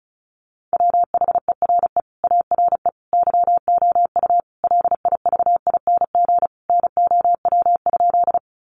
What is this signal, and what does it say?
Morse code, heard as a keyed tone, about 34 wpm, 700 Hz. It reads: WHERE ARE YOU LIVING NOW?
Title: Sentences from Top 1100 Words 35wpm